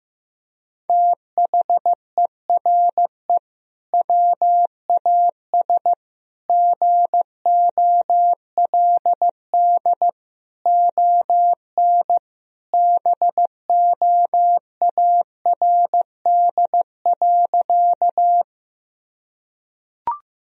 Morse code, heard as a keyed tone, about 15 wpm, 700 Hz. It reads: THERE WAS GOLD ON BOARD.
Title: Sentences from Top 1000 Words - Review Part 4 - 15wpm